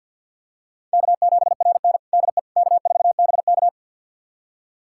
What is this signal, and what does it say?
Morse code, heard as a keyed tone, about 52 wpm, 700 Hz. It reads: K7RN DE K4BK
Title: USA Callsign DE Callsign - 4 Characters - 50wpm